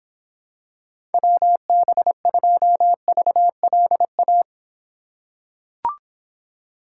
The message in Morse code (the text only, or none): WB2VLA